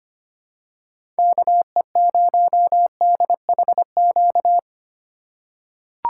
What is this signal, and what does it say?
Morse code, heard as a keyed tone, about 25 wpm, 700 Hz. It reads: KE0DHQ
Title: USA Call Signs - 6 Characters - 3x Character Spacing - 25wpm